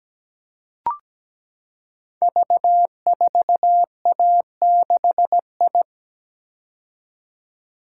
Morse code, heard as a keyed tone, about 17 wpm, 700 Hz. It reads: V4A6I
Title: Code Groups - 5 Letter:Numbers - 17wpm